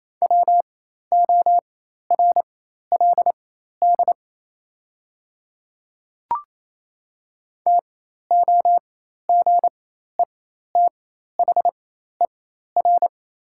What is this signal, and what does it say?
Morse code, heard as a keyed tone, about 28 wpm, spaced as for 13 wpm, 690 Hz. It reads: WORLD TOGETHER
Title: 4x Character Spacing - Top 500 Words - 28wpm